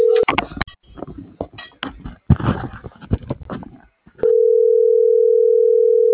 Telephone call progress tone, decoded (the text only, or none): ringback tone